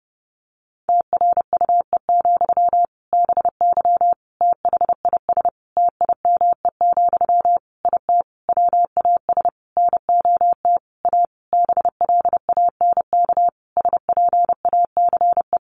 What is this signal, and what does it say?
Morse code, heard as a keyed tone, about 30 wpm, 700 Hz. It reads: TRUE, BY THIS TIME, IT WAS NOT A BLANK SPACE
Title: Sentences from Top 1500 Words 30wpm